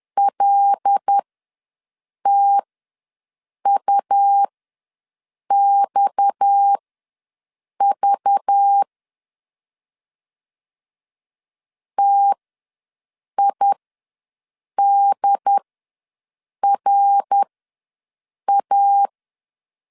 Morse code, heard as a keyed tone, about 10 wpm, spaced as for 6 wpm, 800 Hz. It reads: LTUXV TIDRA
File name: LEZIONI / 6 PKX0 / 6 PKX ws 10wpm 3706